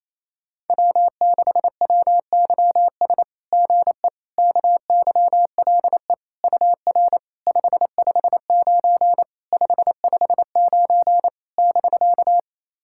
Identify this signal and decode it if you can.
Morse code, heard as a keyed tone, about 28 wpm, 690 Hz. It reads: W6WYS GE KYLE UR 559 559 <BK>
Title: SOTA QSO Practice 28wpm